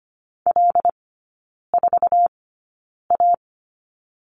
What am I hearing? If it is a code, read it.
Morse code, heard as a keyed tone, about 25 wpm, spaced as for 9 wpm, 700 Hz. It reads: L4A
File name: Lesson 14e - Number 2 - Three Characters 25wpm